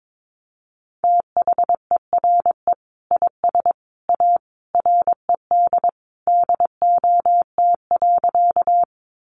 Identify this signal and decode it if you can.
Morse code, heard as a keyed tone, about 22 wpm, 700 Hz. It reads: THERE IS A RED DOT.
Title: Sentences from Top 1500 Words 22wpm